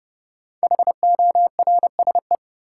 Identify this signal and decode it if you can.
Morse code, heard as a keyed tone, about 30 wpm, 700 Hz. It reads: HORSE
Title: Sentences from Top 300 Words - New Word Drill - 30wpm